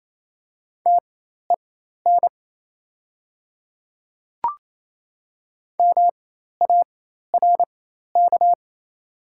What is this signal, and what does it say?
Morse code, heard as a keyed tone, about 28 wpm, spaced as for 13 wpm, 700 Hz. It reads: TEN MARK
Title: 4x Character Spacing - Top 500 Words - 28wpm